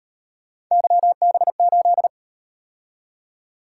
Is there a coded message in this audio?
Yes: Morse code from 0.7 to 2.1 s, heard as a keyed tone, about 38 wpm, 700 Hz. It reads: YB8